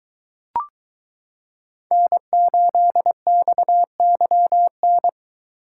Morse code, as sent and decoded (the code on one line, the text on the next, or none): -. ---.. -..- -.-- -.
N8XYN